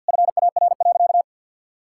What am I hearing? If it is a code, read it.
Morse code, heard as a keyed tone, about 48 wpm, 690 Hz. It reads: FAR.
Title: Sentences from Top 1000 Words - Review Part 5 - 50wpm